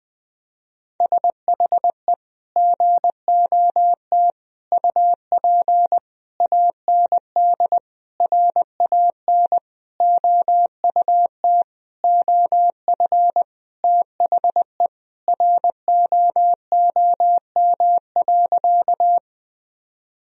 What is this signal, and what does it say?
Morse code, heard as a keyed tone, about 20 wpm, 700 Hz. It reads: SHE GOT UP AND RAN OUT OF THE ROOM.
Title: Sentences from Top 1000 Words - Review Part 3 - 20wpm